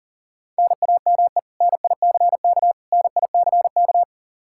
Morse code, heard as a keyed tone, about 37 wpm, 700 Hz. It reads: NAME NICK NICK
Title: Practice QSO - Exchange 1 - Beginning - 40wpm